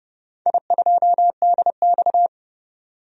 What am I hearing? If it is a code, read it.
Morse code, heard as a keyed tone, about 30 wpm, 700 Hz. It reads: I2DX
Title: DXCC-100 Call Signs - 4 Characters - 30wpm